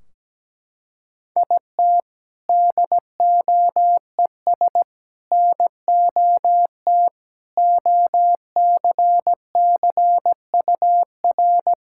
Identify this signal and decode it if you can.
Morse code, heard as a keyed tone, about 17 wpm, 700 Hz. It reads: IT DOES NOT OCCUR
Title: Sentences from Top 1600 Words 17wpm